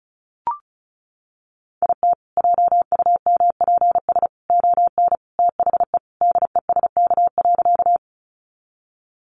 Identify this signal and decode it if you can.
Morse code, heard as a keyed tone, about 34 wpm, 700 Hz. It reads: IT JUMPS ON THE DESK.
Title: Additional Easy Phrases 35wpm